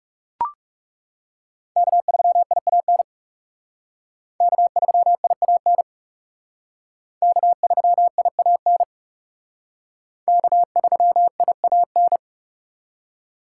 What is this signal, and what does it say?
Morse code, heard as a keyed tone, about 37 wpm, 700 Hz. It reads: K3IAN K3IAN K3IAN K3IAN
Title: Speed Racing - USA 1x3 Call Signs 30wpm